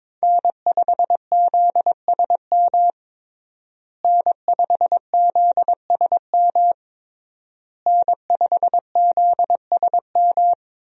Morse code, heard as a keyed tone, about 22 wpm, 700 Hz. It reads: N5ZSM N5ZSM N5ZSM
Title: USA Call Signs - 5 Characters - Repeated 3x - 22wpm